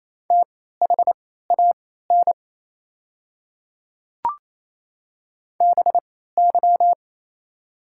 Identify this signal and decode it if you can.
Morse code, heard as a keyed tone, about 27 wpm, spaced as for 16 wpm, 690 Hz. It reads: THAN BY
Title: 3x Character Spacing - Top 200 Words - 28wpm